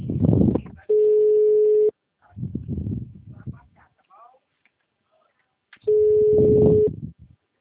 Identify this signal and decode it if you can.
Ringback tone (Europe, 425 Hz), with speech.